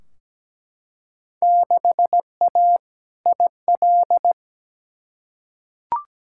Morse code, heard as a keyed tone, about 17 wpm, 700 Hz. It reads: 6A IL
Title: Field Day Exchanges - 17wpm